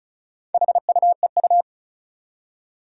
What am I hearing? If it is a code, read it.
Morse code, heard as a keyed tone, about 36 wpm, 700 Hz. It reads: HUEU
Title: Four Challenging Letter-Number Combinations 35wpm